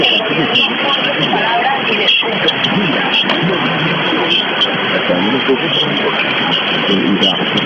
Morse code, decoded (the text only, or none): none